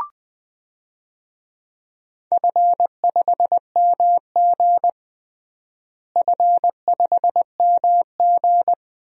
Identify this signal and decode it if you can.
Morse code, heard as a keyed tone, about 19 wpm, 700 Hz. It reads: F5MG F5MG
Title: DXCC-100 Call Signs - 4 Characters - Repeated 2x - 20wpm